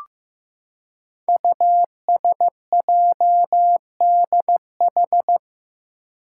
Morse code, heard as a keyed tone, about 15 wpm, 700 Hz. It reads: USJDH
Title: Code Groups - 5 Letters - 15wpm